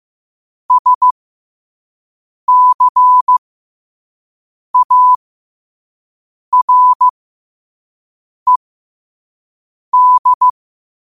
Morse code, heard as a keyed tone, about 15 wpm, spaced as for 5 wpm, 1000 Hz. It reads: SCARED
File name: qotdfiles / 5wpm